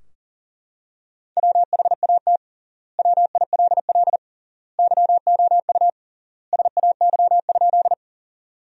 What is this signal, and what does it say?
Morse code, heard as a keyed tone, about 37 wpm, 700 Hz. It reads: WHAT WILL YOU SAY?
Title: 3x Word-Spacing - Sentences from Top 200 Words - 40wpm